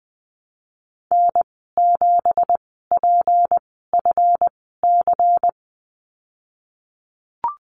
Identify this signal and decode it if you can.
Morse code, heard as a keyed tone, about 20 wpm, spaced as for 15 wpm, 700 Hz. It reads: N7PFC
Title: 193 Lesson 31g - Letter v - Call Signs 20wpm